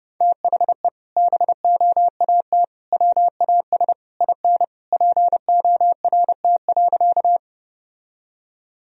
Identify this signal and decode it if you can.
Morse code, heard as a keyed tone, about 30 wpm, 700 Hz. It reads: THE BOAT WAS IN PORT.
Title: Sentences from Top 1000 Words - Review Part 1 - 30wpm